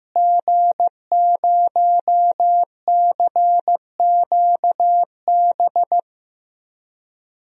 Morse code, heard as a keyed tone, about 15 wpm, 700 Hz. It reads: G0CQB
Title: DXCC-100 Call Signs - 5 Characters - Repeated 2x - 15wpm